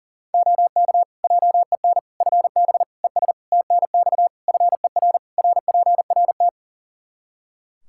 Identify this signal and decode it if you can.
Morse code, heard as a keyed tone, about 40 wpm, 700 Hz. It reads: OK JEN FB ES TNX FER RPRT
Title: Practice QSO - Exchange 2 - Info - 40wpm